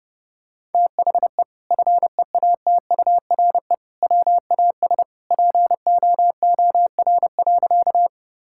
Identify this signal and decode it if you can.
Morse code, heard as a keyed tone, about 30 wpm, 700 Hz. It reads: THE FEATURE WAS POOR.
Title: Sentences from Top 1000 Words - Review Part 5 - 30wpm